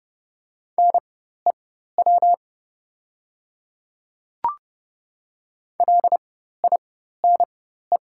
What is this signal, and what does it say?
Morse code, heard as a keyed tone, about 30 wpm, spaced as for 14 wpm, 700 Hz. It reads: NEW LINE
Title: 4x Character Spacing - Top 200 Words - 30wpm